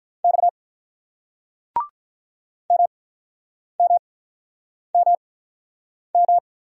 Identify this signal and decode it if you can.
Morse code, heard as a keyed tone, about 42 wpm, spaced as for 9 wpm, 700 Hz. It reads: X MMMM
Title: Speed-Racing Single Letters 35wpm